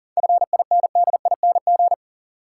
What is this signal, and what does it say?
Morse code, heard as a keyed tone, about 40 wpm, 700 Hz. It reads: FINDING
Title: Sentences from Top 1100 Words - New Word Drill - 40wpm